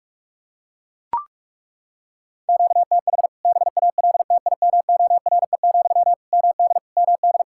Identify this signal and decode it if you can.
Morse code, heard as a keyed tone, about 43 wpm, 700 Hz. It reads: QTH BALTIMORE, MD MD
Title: K1USN SST QSO Practice 45wpm